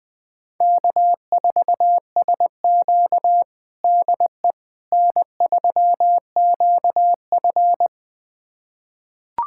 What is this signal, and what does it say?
Morse code, heard as a keyed tone, about 20 wpm, 700 Hz. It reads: K4SQ DE N3QF
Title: USA Callsign DE Callsign - 4 Characters - 20wpm